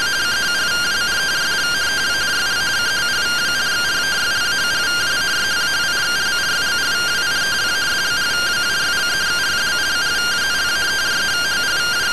Two-tone teleprinter signal, about 170 Hz shift.
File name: findings / MR AXIAL fr1 120.gen-03 rtty rx